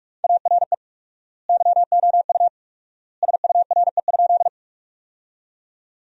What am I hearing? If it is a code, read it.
Morse code, heard as a keyed tone, about 47 wpm, 700 Hz. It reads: ARE YOU SURE?
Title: 4x Word-Spacing - Sentences from Top 300 Words - 45wpm